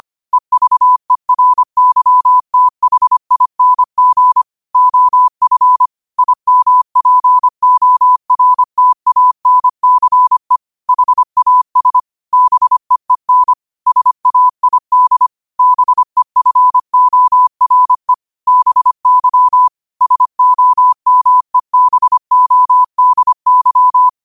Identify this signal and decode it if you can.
Morse code, heard as a keyed tone, about 25 wpm, 1000 Hz. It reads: EVERYTHING OF IMPORTANCE HAS BEEN SAID BEFORE BY SOMEBODY
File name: qotdfiles / 25wpm